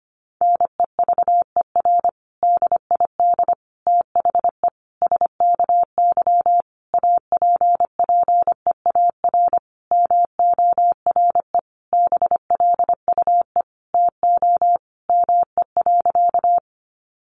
Morse code, heard as a keyed tone, about 25 wpm, 700 Hz. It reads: NEVER DID THE SKY APPEAR MORE BLUE TO ME.
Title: Sentences from Top 1200 Words 25wpm